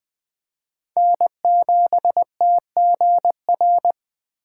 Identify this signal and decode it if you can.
Morse code, heard as a keyed tone, about 21 wpm, 700 Hz. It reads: N7TGR